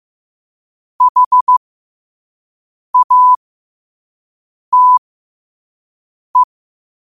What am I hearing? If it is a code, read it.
Morse code, heard as a keyed tone, about 15 wpm, spaced as for 5 wpm, 1000 Hz. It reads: HATE
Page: New quotes every day in morse code at 5 Words per minute.